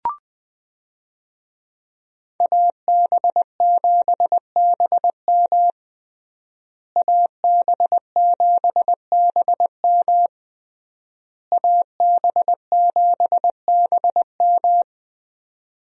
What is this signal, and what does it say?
Morse code, heard as a keyed tone, about 20 wpm, 700 Hz. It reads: AB7BM AB7BM AB7BM
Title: USA Call Signs - 5 Characters - Repeated 3x - 20wpm